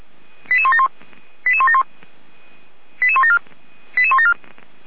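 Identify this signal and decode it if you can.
Five-tone selective call (ZVEI), heard as quick runs of five tones, tones of about 70 ms. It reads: 88171 88171 88174 88174
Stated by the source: ZVEI five-tone selective call received by SDR.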